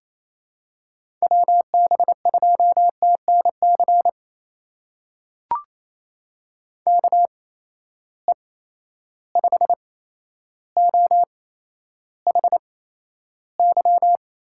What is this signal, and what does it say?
Morse code, heard as a keyed tone, about 28 wpm, spaced as for 12 wpm, 700 Hz. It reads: WB2TNC KE5OHY
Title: USA Call Signs - 6 Characters - 8x Character Spacing - 28wpm